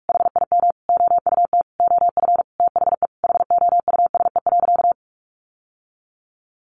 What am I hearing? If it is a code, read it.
Morse code, heard as a keyed tone, about 44 wpm, 700 Hz. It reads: HIM OUT OF THE HOUSE.
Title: Sentences from Top 1300 Words 45wpm